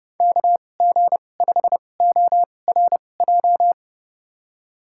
Morse code, heard as a keyed tone, about 31 wpm, spaced as for 22 wpm, 700 Hz. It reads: KG5ORJ
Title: USA Call Signs - 6 Characters - 2x Character Spacing - 30wpm